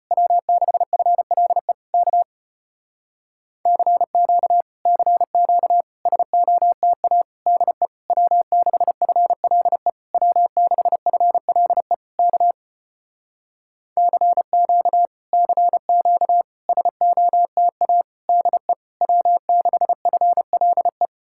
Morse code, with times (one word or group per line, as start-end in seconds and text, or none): 0.1-1.7 W6FLE
1.9-2.2 K
3.7-4.6 CQ
4.9-5.8 CQ
6.1-7.2 SOTA
7.5-7.9 DE
8.1-9.9 W6FLE
10.1-11.9 W6FLE
12.2-12.5 K
14.0-15.1 CQ
15.3-16.4 CQ
16.7-18.0 SOTA
18.3-18.7 DE
19.0-21.1 W6FLE